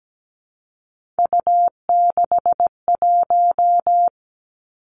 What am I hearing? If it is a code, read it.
Morse code, heard as a keyed tone, about 17 wpm, 700 Hz. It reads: U61
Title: Three Challenging Letter-Number Combinations 17wpm